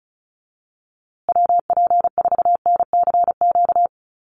Morse code, heard as a keyed tone, about 34 wpm, 700 Hz. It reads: WP4NCQ
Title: USA Call Signs Repeated 2x 35wpm